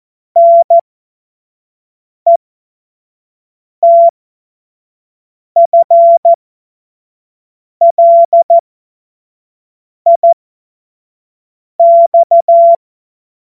Morse code, heard as a keyed tone, about 14 wpm, spaced as for 5 wpm, 680 Hz. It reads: NETFLIX